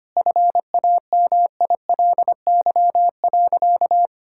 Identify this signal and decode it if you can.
Morse code, heard as a keyed tone, about 25 wpm, 700 Hz. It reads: FAMILY.
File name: practice / 25wpm / Difficult Sentences with Hints - 25wpm